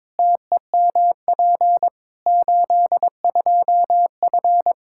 Morse code, heard as a keyed tone, about 22 wpm, 700 Hz. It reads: TEMP 82F
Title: Practice QSO - Exchange 2 - Info - 22wpm